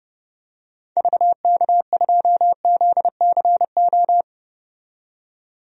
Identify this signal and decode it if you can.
Morse code, heard as a keyed tone, about 30 wpm, 700 Hz. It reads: VK2ZCO
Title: DXCC-100 Call Signs - 6 Characters - 30wpm